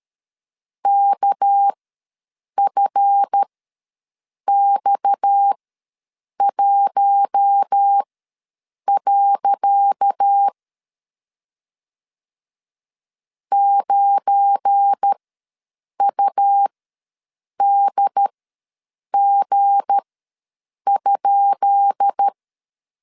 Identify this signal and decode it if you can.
Morse code, heard as a keyed tone, about 13 wpm, spaced as for 7 wpm, 800 Hz. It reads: KFX1. 9UDG?
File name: LEZIONI / 10 SIGNS / 10C DOT ws 12wpm 3548